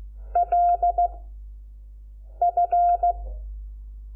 Morse code, heard as a keyed tone, about 16 wpm, 700 Hz. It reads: L F
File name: Lecciones / CLASE Nº5